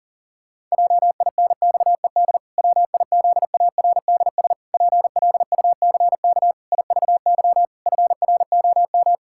Morse code, heard as a keyed tone, about 37 wpm, 700 Hz. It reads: JINXED WIZARDS PLUCK IVY FROM
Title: Pangram Challenge v2 - 40wpm